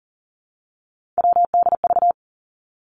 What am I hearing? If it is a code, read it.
Morse code, heard as a keyed tone, about 39 wpm, 700 Hz. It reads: WDV
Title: Three Challenging Letter-Number Combinations 40wpm